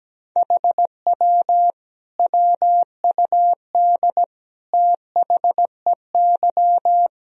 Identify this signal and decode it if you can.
Morse code, heard as a keyed tone, about 17 wpm, 700 Hz. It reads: HW WUD THEY
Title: Sentences from Top 300 Words and Abbreviations 17wpm